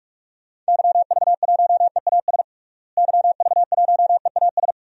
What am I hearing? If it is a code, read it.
Morse code, heard as a keyed tone, about 41 wpm, 700 Hz. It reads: YV1EAS YV1EAS
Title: DXCC-100 Call Signs - 6 Characters - Repeated 2x - 45wpm